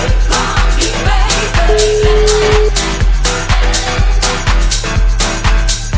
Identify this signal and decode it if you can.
Ringback tone (Europe, 425 Hz), with noise.